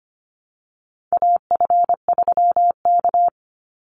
Morse code, heard as a keyed tone, about 25 wpm, 700 Hz. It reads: AF3K
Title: Call Signs Repeated 2x - 25wpm